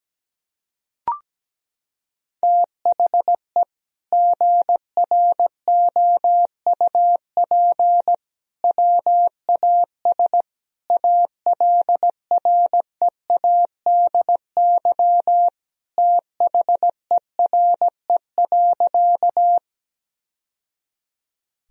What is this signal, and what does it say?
Morse code, heard as a keyed tone, about 17 wpm, 700 Hz. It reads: THE GROUP WAS ALREADY THERE.
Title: Sentences from Top 1000 Words - Review Part 1 - 17wpm